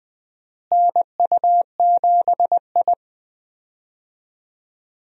Morse code, heard as a keyed tone, about 20 wpm, 700 Hz. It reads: NU7I
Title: USA Call Signs - 4 Characters - Repeated 2x - 20wpm